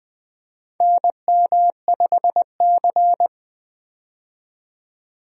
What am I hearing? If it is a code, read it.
Morse code, heard as a keyed tone, about 20 wpm, 700 Hz. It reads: NM5C
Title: USA Call Signs - 4 Characters - 20wpm